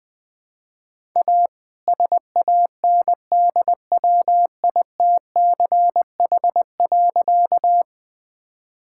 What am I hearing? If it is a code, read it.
Morse code, heard as a keyed tone, about 21 wpm, 700 Hz. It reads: A SANDWITCH.
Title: All-Age Appropriate Jokes - 20wpm